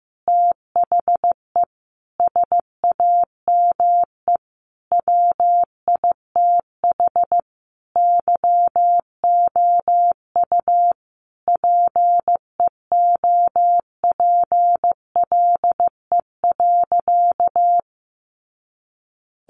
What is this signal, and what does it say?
Morse code, heard as a keyed tone, about 15 wpm, 700 Hz. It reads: THE SAME WITH YOU PEOPLE.
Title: Sentences from Top 300 Words 15wpm